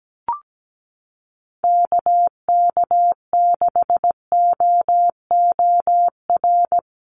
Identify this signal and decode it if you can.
Morse code, heard as a keyed tone, about 17 wpm, 700 Hz. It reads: KK6OOR
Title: USA Call Signs Repeated 3x 17wpm